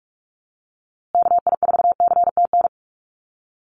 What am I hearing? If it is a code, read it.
Morse code, heard as a keyed tone, about 44 wpm, 700 Hz. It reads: KI4CTN
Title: USA Call Signs Repeated 2x 45wpm